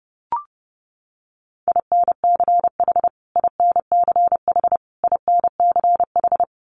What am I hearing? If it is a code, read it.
Morse code, heard as a keyed tone, about 30 wpm, 700 Hz. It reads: INCH INCH INCH
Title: Top 500 Words Repeated 30wpm